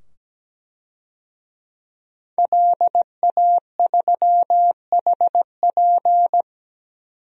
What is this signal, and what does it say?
Morse code, heard as a keyed tone, about 17 wpm, 700 Hz. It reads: LA3HP
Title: DXCC-100 Call Signs - 5 Characters - 17wpm